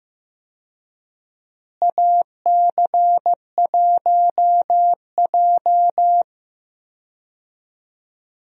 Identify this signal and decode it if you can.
Morse code, heard as a keyed tone, about 15 wpm, 700 Hz. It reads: AC1J
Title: USA Call Signs - 4 Characters - Repeated 3x - 15wpm